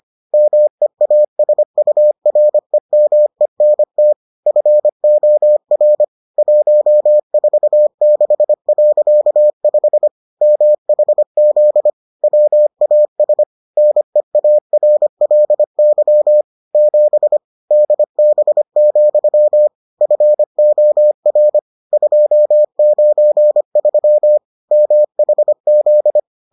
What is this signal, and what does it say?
Morse code, heard as a keyed tone, about 25 wpm, 600 Hz. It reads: MEASUREMENT FOR 146.5 MHZ WAS NEARLY 7 DB, FOR 293 MHZ